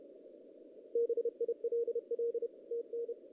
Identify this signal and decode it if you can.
Morse code, heard as a keyed tone, about 29 wpm, 410 Hz. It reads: BILL TN
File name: 2016 JAN CW NAQP / 2016 Jan naqp k5ot - 0-5